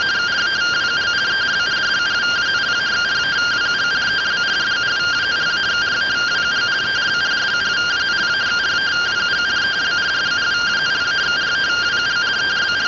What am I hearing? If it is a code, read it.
Two-tone teleprinter signal, about 170 Hz shift.
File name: findings / 202 coronal iodine 3mm.gen-33 rtty rx